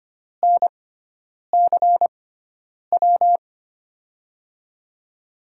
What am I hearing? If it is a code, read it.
Morse code, heard as a keyed tone, about 25 wpm, spaced as for 9 wpm, 700 Hz. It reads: NCW
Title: USA Call Signs - 6 Characters - 6x Character Spacing - 25wpm